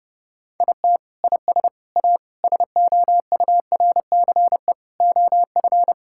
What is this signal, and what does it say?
Morse code, heard as a keyed tone, about 30 wpm, 700 Hz. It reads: IT IS A SOURCE OF
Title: Sentences from Top 1000 Words - Review Part 5 - 30wpm